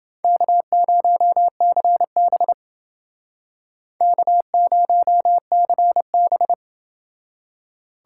Morse code, heard as a keyed tone, about 29 wpm, 700 Hz. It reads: K0CB K0CB